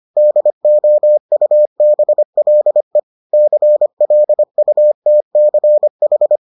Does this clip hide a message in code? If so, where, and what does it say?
Yes: Morse code from 0.2 to 6.4 s, heard as a keyed tone, about 25 wpm, 600 Hz. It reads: DOUBLE CLUTCH